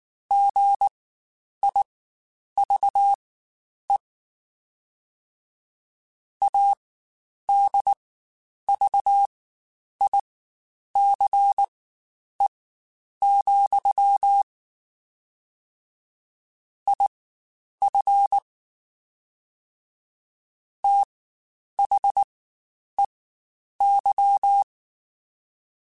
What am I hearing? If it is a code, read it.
Morse code, heard as a keyed tone, about 19 wpm, spaced as for 9 wpm, 800 Hz. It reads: GIVE ADVICE, IF THEY
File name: Quotes Spaced / Quotes Spaced-mp3s / Advice sp @18